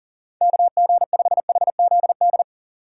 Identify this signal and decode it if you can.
Morse code, heard as a keyed tone, about 40 wpm, 700 Hz. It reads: KG5HZD
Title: USA Call Signs - 6 Characters - 6x Character Spacing - 40wpm